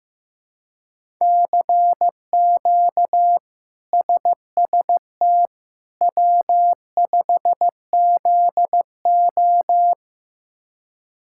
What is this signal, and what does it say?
Morse code, heard as a keyed tone, about 15 wpm, 700 Hz. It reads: CQ SST W5ZO
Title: K1USN SST QSO Practice - 15wpm